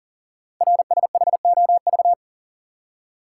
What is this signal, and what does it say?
Morse code, heard as a keyed tone, about 41 wpm, 690 Hz. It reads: RSHOV
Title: Code Groups - 5 Letter:Numbers - 40wpm